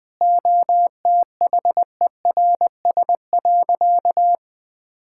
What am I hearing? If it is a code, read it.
Morse code, heard as a keyed tone, about 20 wpm, 700 Hz. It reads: OTHERS.
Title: Sentences from Top 1000 Words - Review Part 6 - 20wpm